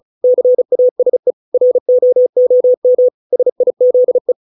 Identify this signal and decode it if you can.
Morse code, heard as a keyed tone, about 34 wpm, 500 Hz. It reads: CASE ROOM SIZE